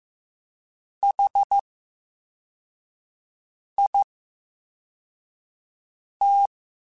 Morse code, heard as a keyed tone, about 15 wpm, 800 Hz. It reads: H I T
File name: Special INGs-mp3s / Special INGs-mp3s / VerySlowToFaster Ing @13